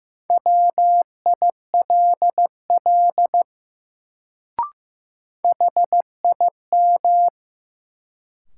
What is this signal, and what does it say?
Morse code, heard as a keyed tone, about 15 wpm, 700 Hz. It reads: WILL HIM
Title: Sentences from Top 100 Words - New Word Drill - 15wpm